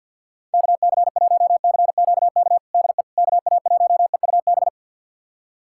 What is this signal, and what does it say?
Morse code, heard as a keyed tone, about 51 wpm, 700 Hz. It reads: KC1CQK DE KA1EUB
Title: USA Callsign DE Callsign 50wpm